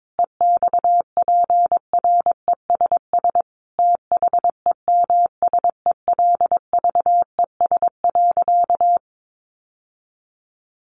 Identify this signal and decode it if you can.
Morse code, heard as a keyed tone, about 22 wpm, 700 Hz. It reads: EXPRESS THEMSELVES.
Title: Sentences from Top 1100 Words 22wpm